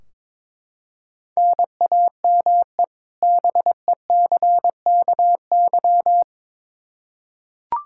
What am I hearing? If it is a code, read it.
Morse code, heard as a keyed tone, about 22 wpm, 700 Hz. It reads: NAME BECKY
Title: Practice QSO - Exchange 1 - Beginning - 22wpm